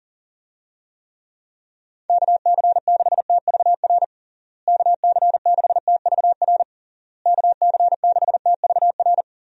Morse code, heard as a keyed tone, about 40 wpm, 700 Hz. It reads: KC6TVR KC6TVR KC6TVR
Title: USA Call Signs - 6 Characters - Repeated 3x - 40wpm